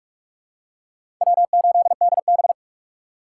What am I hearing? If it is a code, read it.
Morse code, heard as a keyed tone, about 46 wpm, 700 Hz. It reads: W8DB
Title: USA Call Signs - 4 Characters - 6x Character Spacing - 45wpm